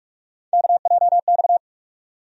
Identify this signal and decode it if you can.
Morse code, heard as a keyed tone, about 46 wpm, 700 Hz. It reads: KJX
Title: USA Call Sign Postfixes - 3 Letters 45wpm